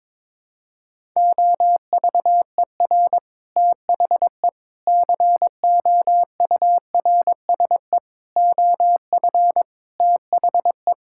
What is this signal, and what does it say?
Morse code, heard as a keyed tone, about 21 wpm, 700 Hz. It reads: OVER THE COURSE OF THE
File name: practice / 22wpm / Sentences with 4-Word Phrases - 22wpm